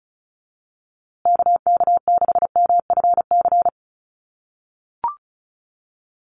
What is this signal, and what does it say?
Morse code, heard as a keyed tone, about 34 wpm, 700 Hz. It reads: KK6MFC
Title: USA Call Signs Repeated 2x 35wpm